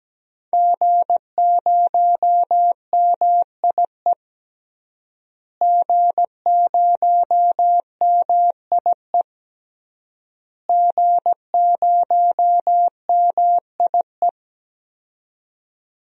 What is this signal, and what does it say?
Morse code, heard as a keyed tone, about 17 wpm, 700 Hz. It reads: G0MIE G0MIE G0MIE
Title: DXCC-100 Call Signs - 5 Characters - Repeated 3x - 17wpm